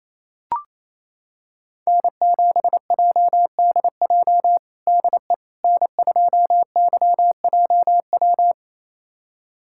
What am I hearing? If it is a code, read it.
Morse code, heard as a keyed tone, about 28 wpm, 700 Hz. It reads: N7JDJ DE N2YJW
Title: USA Callsign DE Callsign 28wpm